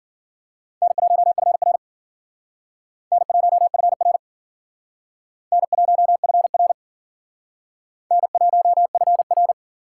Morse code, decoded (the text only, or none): N1FR N1FR N1FR N1FR